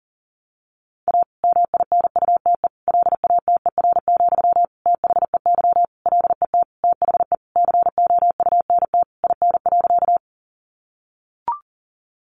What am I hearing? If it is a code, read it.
Morse code, heard as a keyed tone, about 40 wpm, 700 Hz. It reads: A MINUTE LATER, THEY LET THE COUNT IN.
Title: Sentences from Top 1500 Words 40wpm